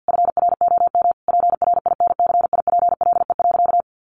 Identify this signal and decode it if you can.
Morse code, heard as a keyed tone, about 50 wpm, 700 Hz. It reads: FROM PRINCIPLE.
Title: Sentences from Top 1400 Words 50wpm